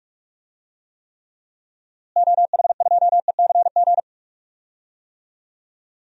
Morse code, heard as a keyed tone, about 46 wpm, 700 Hz. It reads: OH2ECG
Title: DXCC-100 Call Signs - 6 Characters - Repeated 3x - 45wpm